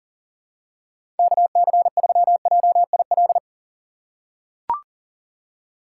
Morse code, heard as a keyed tone, about 40 wpm, 700 Hz. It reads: KC3JIL
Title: USA Call Signs - 6 Characters - 6x Character Spacing - 40wpm